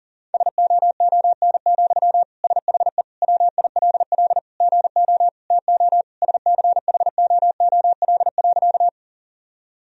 Morse code, heard as a keyed tone, about 40 wpm, 700 Hz. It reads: SOON, SHE WILL GO TO SCHOOL.
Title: Sentences from Top 1000 Words - Review Part 1 - 40wpm